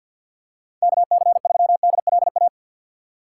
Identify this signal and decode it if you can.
Morse code, heard as a keyed tone, about 48 wpm, 700 Hz. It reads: KC3DLA
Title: USA Call Signs - 3x Character Spacing - 50wpm